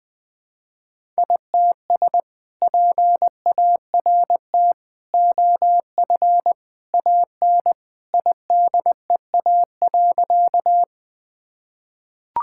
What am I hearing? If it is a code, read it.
Morse code, heard as a keyed tone, about 20 wpm, 700 Hz. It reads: ITS PART OF AN IDEA.
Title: Sentences with 2-Word Phrases - 20wpm